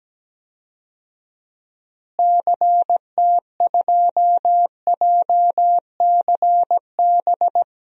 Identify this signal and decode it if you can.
Morse code, heard as a keyed tone, about 17 wpm, 700 Hz. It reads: CT2JCB
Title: DXCC-100 Call Signs - 6 Characters - Repeated 2x - 17wpm